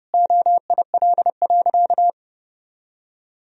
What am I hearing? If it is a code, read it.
Morse code, heard as a keyed tone, about 30 wpm, 700 Hz. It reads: OIL.
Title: Sentences from Top 1000 Words - Review Part 4 - 30wpm